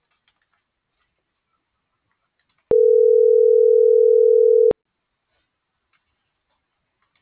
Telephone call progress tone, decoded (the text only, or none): ringback tone